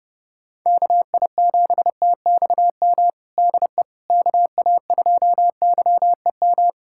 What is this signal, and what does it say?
Morse code, heard as a keyed tone, about 31 wpm, 700 Hz. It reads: KI7TXM DE KA2YEM
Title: USA Callsign DE Callsign - 6 Characters - 30wpm